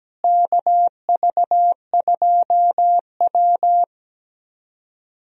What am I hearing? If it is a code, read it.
Morse code, heard as a keyed tone, about 17 wpm, 700 Hz. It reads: KV2W
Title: USA Call Signs - 4 Characters - Repeated 2x - 17wpm